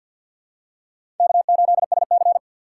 Morse code, heard as a keyed tone, about 49 wpm, 700 Hz. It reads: K8SC
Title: USA Call Signs - 4 Characters - 6x Character Spacing - 50wpm